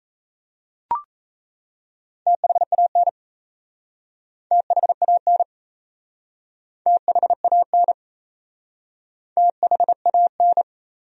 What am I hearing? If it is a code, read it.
Morse code, heard as a keyed tone, about 37 wpm, 700 Hz. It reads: THAN THAN THAN THAN